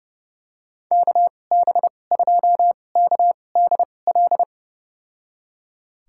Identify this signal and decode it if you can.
Morse code, heard as a keyed tone, about 31 wpm, spaced as for 22 wpm, 700 Hz. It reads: KB2KDL
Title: USA Call Signs - 6 Characters - 2x Character Spacing - 30wpm